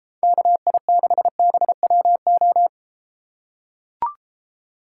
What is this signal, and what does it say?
Morse code, heard as a keyed tone, about 32 wpm, 700 Hz. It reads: KI6BWO